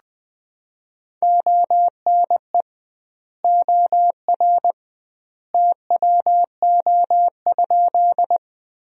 Morse code, heard as a keyed tone, about 20 wpm, 700 Hz. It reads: ONE OR TWO?
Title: 2x Word-Spacing - Sentences from Top 100 Words - 20wpm